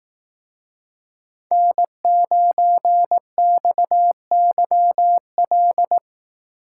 Morse code, heard as a keyed tone, about 18 wpm, 700 Hz. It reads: N9XYL
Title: Speed Racing - USA 1x3 Call Signs 15wpm